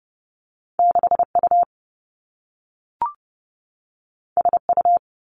Morse code, heard as a keyed tone, about 30 wpm, 700 Hz. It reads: 6 U S U
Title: Two Challenging Letter-Number Combinations 30wpm